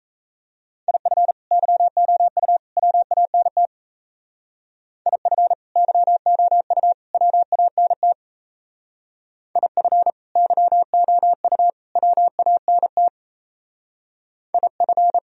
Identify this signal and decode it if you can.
Morse code, heard as a keyed tone, about 35 wpm, 700 Hz. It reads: IF YOU WANT IF YOU WANT IF YOU WANT IF